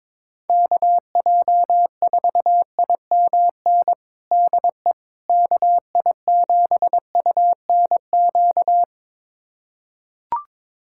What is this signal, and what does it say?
Morse code, heard as a keyed tone, about 21 wpm, 700 Hz. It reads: KJ4IMN DE KI7UNQ
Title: USA Callsign DE Callsign - 6 Characters - 22wpm